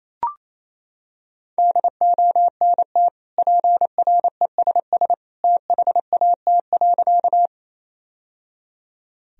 Morse code, heard as a keyed tone, about 28 wpm, 700 Hz. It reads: DONT PRESS THAT.
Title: Sentences from Top 1000 Words - Review Part 1 - 28wpm